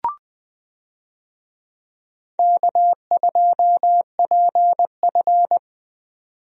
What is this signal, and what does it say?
Morse code, heard as a keyed tone, about 20 wpm, 700 Hz. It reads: K2PF
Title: USA Call Signs - 4 Characters - 20wpm